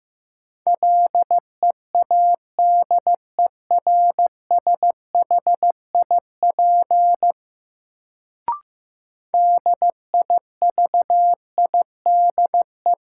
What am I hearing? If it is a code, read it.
Morse code, heard as a keyed tone, about 15 wpm, 700 Hz. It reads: LEADERSHIP DIVIDE
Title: Sentences from Top 1600 Words - New Word Drill Alt - 15wpm